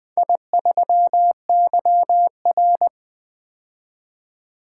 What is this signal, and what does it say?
Morse code, heard as a keyed tone, about 20 wpm, 700 Hz. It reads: I3YR
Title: Four Letter-Number Combinations - 20wpm